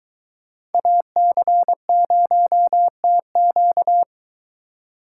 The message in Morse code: AC0TQ

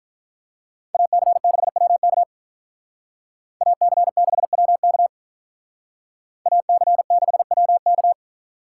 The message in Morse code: AC6WK AC6WK AC6WK

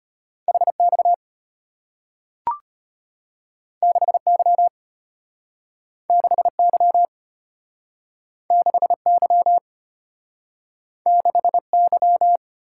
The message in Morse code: HX 6Y 6Y 6Y 6Y

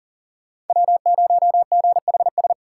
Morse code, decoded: W0GHS

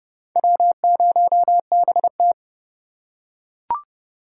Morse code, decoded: W0BT

